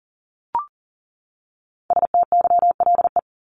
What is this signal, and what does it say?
Morse code, heard as a keyed tone, about 40 wpm, 700 Hz. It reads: STYLE
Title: Top 751-1000 Words 40wpm